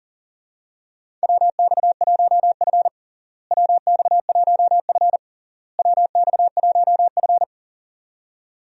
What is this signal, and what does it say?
Morse code, heard as a keyed tone, about 40 wpm, 700 Hz. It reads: WX1F WX1F WX1F
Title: USA Call Signs - 4 Characters - Repeated 3x - 40wpm